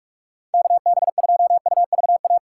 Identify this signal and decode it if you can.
Morse code, heard as a keyed tone, about 46 wpm, 700 Hz. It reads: KB2UVA
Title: USA Call Signs - 6 Characters - 6x Character Spacing - 45wpm